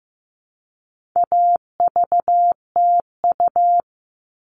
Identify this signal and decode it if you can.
Morse code, heard as a keyed tone, about 15 wpm, 700 Hz. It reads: AVTU
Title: All Cut Numbers - 4-Digit 15wpm